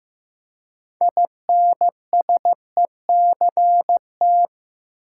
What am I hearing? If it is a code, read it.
Morse code, heard as a keyed tone, about 15 wpm, 700 Hz. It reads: INSECT
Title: Six Letter Words - 15wpm